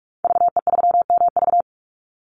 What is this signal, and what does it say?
Morse code, heard as a keyed tone, about 44 wpm, 700 Hz. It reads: VE3MV
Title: Call Signs - 45wpm